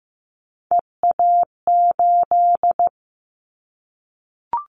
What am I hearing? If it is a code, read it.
Morse code, heard as a keyed tone, about 15 wpm, 700 Hz. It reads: EA8
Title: Callsign Prefixes for Top 100 DXCC - 15wpm